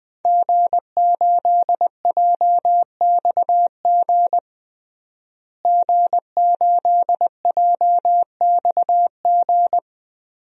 Morse code, heard as a keyed tone, about 20 wpm, 700 Hz. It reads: G8JXG G8JXG
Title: DXCC-100 Call Signs - 5 Characters - Repeated 2x - 20wpm